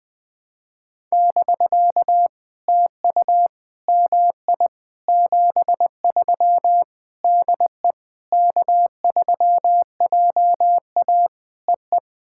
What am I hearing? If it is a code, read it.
Morse code, heard as a keyed tone, about 20 wpm, 700 Hz. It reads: <BK> TU MI 73 DE K3JA EE
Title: POTA QSO Practice - 20wpm